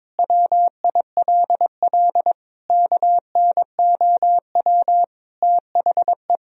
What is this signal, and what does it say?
Morse code, heard as a keyed tone, about 22 wpm, 700 Hz. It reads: WILL KNOW THE
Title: Sentences from Top 1000 Words - Review Part 3 - 22wpm